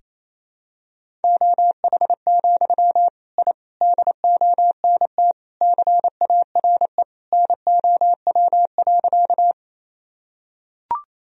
Morse code, heard as a keyed tone, about 28 wpm, 700 Hz. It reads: OH, I DONT CARE NOW.
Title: Sentences from Top 1000 Words - Review Part 1 - 28wpm